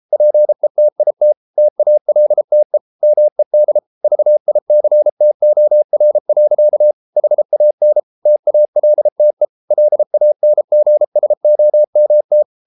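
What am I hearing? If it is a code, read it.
Morse code, heard as a keyed tone, about 33 wpm, 600 Hz. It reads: PETIT TALTE MED VICTOR. HAN TALTE LANGSOMT
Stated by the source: Morse code at 33 wpm, QRQ.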